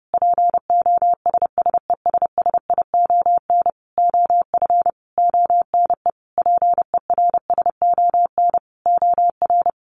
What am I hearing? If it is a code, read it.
Morse code, heard as a keyed tone, about 30 wpm, 700 Hz. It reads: POSSESSION OF ONE PERSON OR
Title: Sentences from Top 1400 Words 30wpm